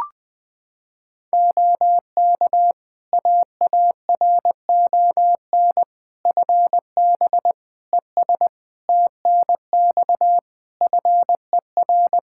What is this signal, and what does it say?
Morse code, heard as a keyed tone, about 20 wpm, 700 Hz. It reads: OK AARON FB ES TNX FER
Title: K1USN SST QSO Practice 20wpm